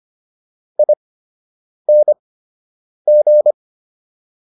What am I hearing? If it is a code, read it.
Morse code, heard as a keyed tone, about 25 wpm, spaced as for 8 wpm, 600 Hz. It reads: ING